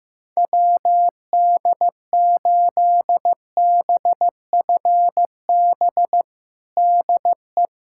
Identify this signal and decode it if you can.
Morse code, heard as a keyed tone, about 15 wpm, 700 Hz. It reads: WD8BFB DE
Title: USA Callsign DE Callsign - 6 Characters - 15wpm